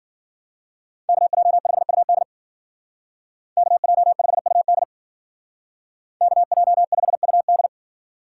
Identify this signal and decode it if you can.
Morse code, heard as a keyed tone, about 50 wpm, 700 Hz. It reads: KJ5UD KJ5UD KJ5UD
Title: Speed Racing - USA 2x2 Call Signs 40wpm